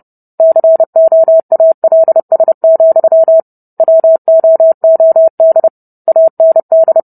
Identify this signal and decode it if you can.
Morse code, heard as a keyed tone, about 30 wpm, 650 Hz. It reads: COALS, WOOD AND